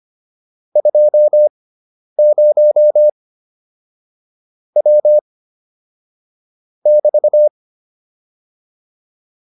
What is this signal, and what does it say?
Morse code, heard as a keyed tone, about 25 wpm, spaced as for 10 wpm, 600 Hz. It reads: 20 W <BT>